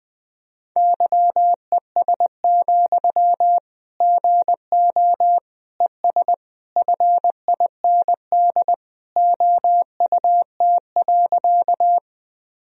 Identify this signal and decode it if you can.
Morse code, heard as a keyed tone, about 20 wpm, 700 Hz. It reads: YES, GO ES FIND OUT.
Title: Sentences from Top 500 Words and Abbreviations 20wpm